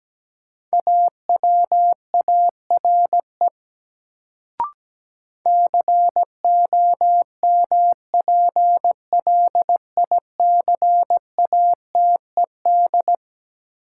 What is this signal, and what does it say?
Morse code, heard as a keyed tone, about 17 wpm, 700 Hz. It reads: AWARE COMPLICATED
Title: Sentences from Top 1200 Words - New Word Drill - 17wpm